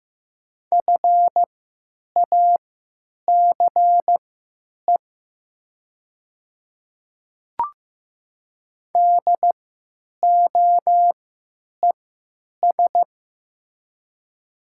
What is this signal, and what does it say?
Morse code, heard as a keyed tone, about 15 wpm, spaced as for 9 wpm, 700 Hz. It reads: FACE DOES
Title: 3x Character Spacing - Top 300 Words - 15wpm